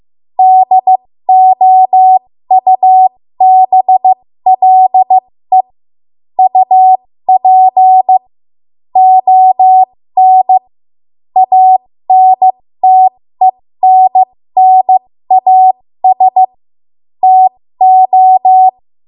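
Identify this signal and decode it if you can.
Morse code, heard as a keyed tone, about 15 wpm, spaced as for 13 wpm, 750 Hz. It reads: DOUBLE UP ON ANTENNAS TO
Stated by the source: 13 wpm Morse code practice archive transmission.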